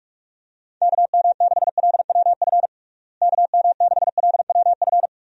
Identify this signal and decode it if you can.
Morse code, heard as a keyed tone, about 46 wpm, 700 Hz. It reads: KM6LWF KM6LWF
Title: USA Call Signs - 6 Characters - Repeated 2x - 45wpm